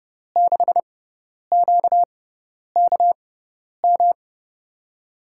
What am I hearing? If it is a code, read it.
Morse code, heard as a keyed tone, about 30 wpm, spaced as for 10 wpm, 700 Hz. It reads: 6QKM
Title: USA Call Signs - 6 Characters - 6x Character Spacing - 30wpm